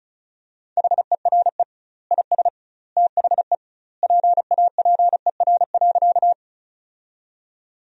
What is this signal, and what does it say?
Morse code, heard as a keyed tone, about 36 wpm, 700 Hz. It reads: HERE IS THE PAPER.
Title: 2x Word-Spacing - Sentences from Top 300 Words - 35wpm